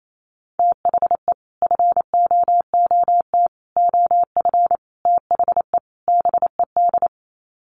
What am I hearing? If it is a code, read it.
Morse code, heard as a keyed tone, about 28 wpm, 700 Hz. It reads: THE FOOT OF THE BED
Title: Sets of 5 Words Encore2 28wpm